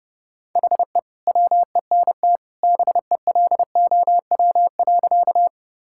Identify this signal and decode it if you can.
Morse code, heard as a keyed tone, about 32 wpm, 700 Hz. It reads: HE WENT BELOW.
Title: Sentences from Top 1000 Words - Review Part 5 - 30wpm